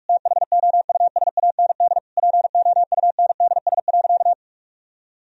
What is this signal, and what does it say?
Morse code, heard as a keyed tone, about 46 wpm, 700 Hz. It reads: THOUSAND POUNDS.
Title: Sentences from Top 1000 Words - Review Part 3 - 45wpm